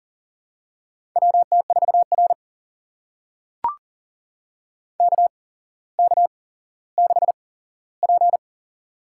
Morse code, heard as a keyed tone, about 40 wpm, spaced as for 17 wpm, 700 Hz. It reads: WT4R KK6P
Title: USA Call Signs - 4 Characters - 8x Character Spacing - 40wpm